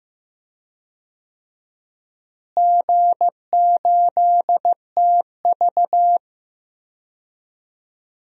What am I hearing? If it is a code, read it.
Morse code, heard as a keyed tone, about 16 wpm, 700 Hz. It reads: G8TV
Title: DXCC-100 Call Signs - 4 Characters - Repeated 2x - 15wpm